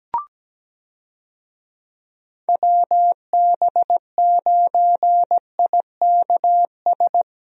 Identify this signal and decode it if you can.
Morse code, heard as a keyed tone, about 17 wpm, 700 Hz. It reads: WB9IKS
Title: USA Call Signs - 6 Characters - Repeated 2x - 17wpm